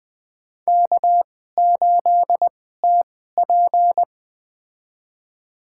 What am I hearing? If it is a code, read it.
Morse code, heard as a keyed tone, about 20 wpm, spaced as for 14 wpm, 700 Hz. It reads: K8TP
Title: USA Call Signs - 4 Characters - 2x Character Spacing - 20wpm